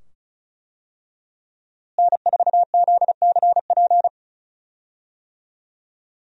Morse code, heard as a keyed tone, about 36 wpm, 700 Hz. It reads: N4ZCP
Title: USA Call Signs - 5 Characters - Repeated 2x - 35wpm